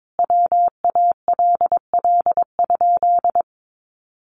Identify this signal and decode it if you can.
Morse code, heard as a keyed tone, about 22 wpm, 700 Hz. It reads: WALL?
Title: Sentences from Top 1100 Words 22wpm